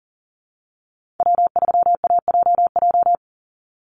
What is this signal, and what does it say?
Morse code, heard as a keyed tone, about 40 wpm, 700 Hz. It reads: W3AJJ
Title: USA Call Signs 40wpm